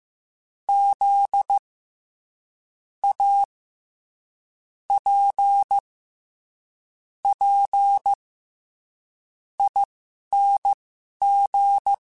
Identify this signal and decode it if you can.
Morse code, heard as a keyed tone, about 15 wpm, spaced as for 11 wpm, 800 Hz. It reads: Z A P P ING
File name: Special INGs-mp3s / Special INGs-mp3s / SlowToFaster ing @10